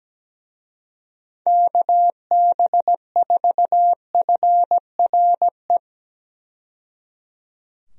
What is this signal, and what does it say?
Morse code, heard as a keyed tone, about 17 wpm, 700 Hz. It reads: KB4FRE
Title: USA Call Signs - 6 Characters - 17wpm